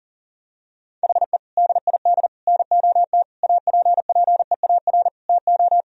Morse code, heard as a keyed tone, about 40 wpm, 700 Hz. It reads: HE DID NOT APPEAR TO
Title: Sentences from Top 1000 Words - Review Part 2 - 40wpm